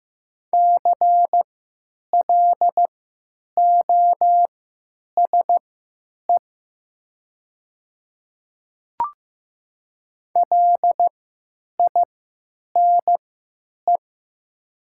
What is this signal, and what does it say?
Morse code, heard as a keyed tone, about 15 wpm, spaced as for 9 wpm, 700 Hz. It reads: CLOSE LINE
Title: 3x Character Spacing - Top 300 Words - 15wpm